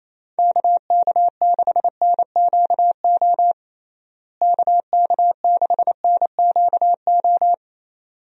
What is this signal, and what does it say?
Morse code, heard as a keyed tone, about 28 wpm, 700 Hz. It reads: KK6NQO KK6NQO
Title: USA Call Signs - 6 Characters - Repeated 2x - 28wpm